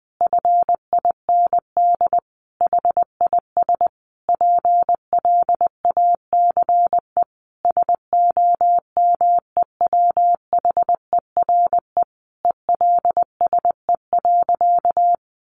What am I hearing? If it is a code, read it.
Morse code, heard as a keyed tone, about 20 wpm, 700 Hz. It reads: FIND HIS PLACE SOMEWHERE ELSE.
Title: Sentences from Top 1400 Words 20wpm